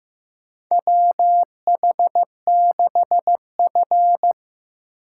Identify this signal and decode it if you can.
Morse code, heard as a keyed tone, about 15 wpm, 700 Hz. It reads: WH6F
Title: USA Call Signs - 4 Characters - 4x Character Spacing - 15wpm